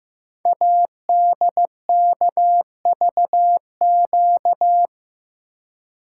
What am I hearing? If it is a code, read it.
Morse code, heard as a keyed tone, about 15 wpm, 700 Hz. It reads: ADKVQ
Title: Code Groups - 5 Letters - 15wpm